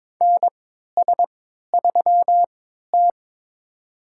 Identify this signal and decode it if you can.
Morse code, heard as a keyed tone, about 22 wpm, spaced as for 13 wpm, 700 Hz. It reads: NS3T
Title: USA Call Signs - 3x Character Spacing - 22wpm